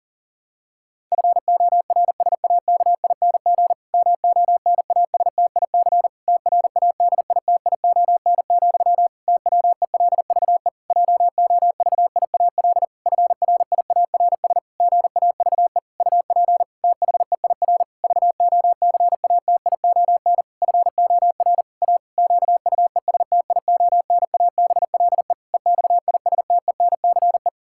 Morse code, heard as a keyed tone, about 37 wpm, 700 Hz. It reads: FORSAKING MONASTIC TRADITION, TWELVE JOVIAL FRIARS GAVE UP THEIR VOCATION FOR A QUESTIONABLE EXISTENCE
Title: Pangram Challenge v2 - 40wpm